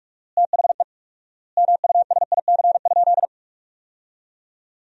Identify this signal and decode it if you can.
Morse code, heard as a keyed tone, about 46 wpm, 700 Hz. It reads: THE MUSIC?
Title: 4x Word-Spacing - Sentences from Top 300 Words - 45wpm